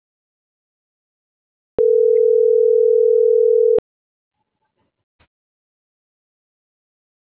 Ringback tone (North America), 440 + 480 Hz.